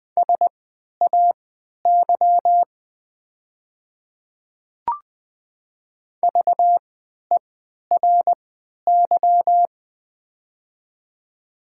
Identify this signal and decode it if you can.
Morse code, heard as a keyed tone, about 21 wpm, spaced as for 12 wpm, 700 Hz. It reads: SAY VERY
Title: 3x Character Spacing - Top 200 Words - 20wpm